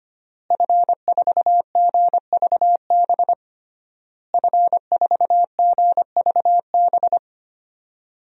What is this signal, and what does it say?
Morse code, heard as a keyed tone, about 25 wpm, 700 Hz. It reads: F4GVB F4GVB
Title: DXCC-100 Call Signs - 5 Characters - Repeated 3x - 25wpm